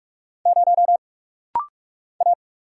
Morse code, heard as a keyed tone, about 46 wpm, 700 Hz. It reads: 0 A
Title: Single Letter-Number - Rapid-Fire 45wpm